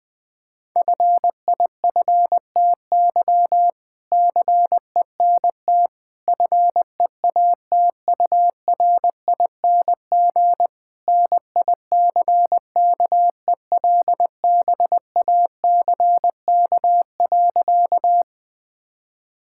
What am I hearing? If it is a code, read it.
Morse code, heard as a keyed tone, about 20 wpm, 700 Hz. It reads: FIFTY CENT FEATURING NICKELBACK.
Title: All-Age Appropriate Jokes - 20wpm